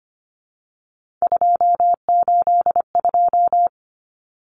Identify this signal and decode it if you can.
Morse code, heard as a keyed tone, about 25 wpm, 700 Hz. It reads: 282
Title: Numbers - 3-Digit 25wpm